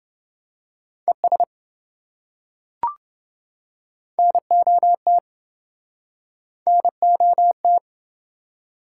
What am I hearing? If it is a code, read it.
Morse code, heard as a keyed tone, about 27 wpm, 700 Hz. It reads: ES NOT NOT